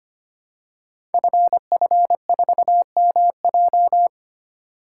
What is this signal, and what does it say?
Morse code, heard as a keyed tone, about 25 wpm, 700 Hz. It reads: FF4MJ
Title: Code Groups - 5 Letter:Numbers - 25wpm